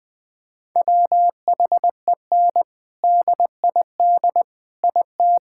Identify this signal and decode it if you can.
Morse code, heard as a keyed tone, about 19 wpm, 700 Hz. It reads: WHEN DID IT
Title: Sentences from Top 1000 Words - Review Part 6 - 20wpm